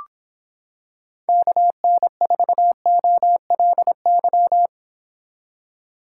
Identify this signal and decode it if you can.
Morse code, heard as a keyed tone, about 26 wpm, 700 Hz. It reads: KN4OLY